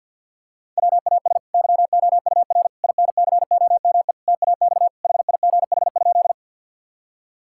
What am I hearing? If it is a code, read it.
Morse code, heard as a keyed tone, about 50 wpm, 700 Hz. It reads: WAS YOUR INCOME TAX HIGH?
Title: Sentences from Top 1000 Words - Review Part 5 - 50wpm